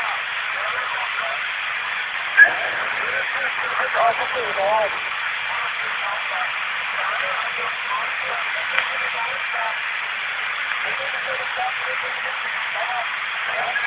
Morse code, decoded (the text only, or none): none